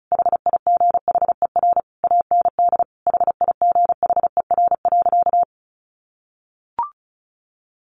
Morse code, heard as a keyed tone, about 34 wpm, 700 Hz. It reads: HIGHER AND HIGHER.
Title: Additional Easy Phrases 35wpm